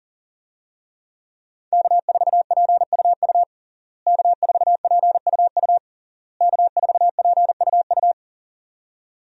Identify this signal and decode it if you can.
Morse code, heard as a keyed tone, about 37 wpm, 700 Hz. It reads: K4PUU K4PUU K4PUU
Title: USA Call Signs - 5 Characters - Repeated 3x - 40wpm